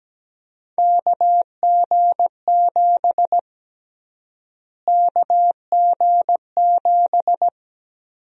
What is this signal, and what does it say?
Morse code, heard as a keyed tone, about 17 wpm, 700 Hz. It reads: KG7 KG7
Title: USA Call Sign Prefixes - Repeated 2x - 17wpm